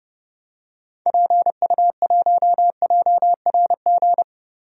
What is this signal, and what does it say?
Morse code, heard as a keyed tone, about 30 wpm, 700 Hz. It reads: PU1JRG
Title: DXCC-100 Call Signs - 6 Characters - Repeated 2x - 30wpm